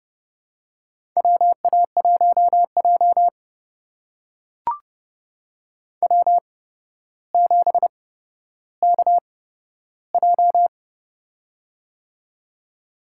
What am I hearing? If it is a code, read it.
Morse code, heard as a keyed tone, about 31 wpm, spaced as for 13 wpm, 700 Hz. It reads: WA1J W7KJ
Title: USA Call Signs - 4 Characters - 8x Character Spacing - 30wpm